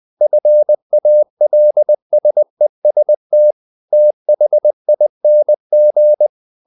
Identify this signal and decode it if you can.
Morse code, heard as a keyed tone, about 20 wpm, 600 Hz. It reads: FALSEST THING
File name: the old man and the sea / the old man and the sea part 1 20wpm